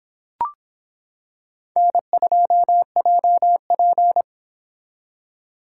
Morse code, heard as a keyed tone, about 25 wpm, 700 Hz. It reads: N2JP